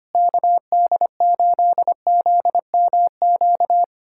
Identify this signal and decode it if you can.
Morse code, heard as a keyed tone, about 25 wpm, 700 Hz. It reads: KD8ZMQ
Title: USA Callsign DE Callsign 25wpm